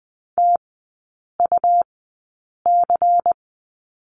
Morse code, heard as a keyed tone, about 20 wpm, spaced as for 8 wpm, 700 Hz. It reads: TUC
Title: 114 Lesson 19e - Letter w - Three Characters 20wpm